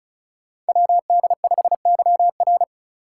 Morse code, heard as a keyed tone, about 34 wpm, 700 Hz. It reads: WD5YR
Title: Code Groups - 5 Letter:Numbers - 35wpm